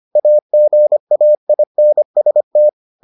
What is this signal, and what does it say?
Morse code, heard as a keyed tone, about 25 wpm, 600 Hz. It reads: AGAINST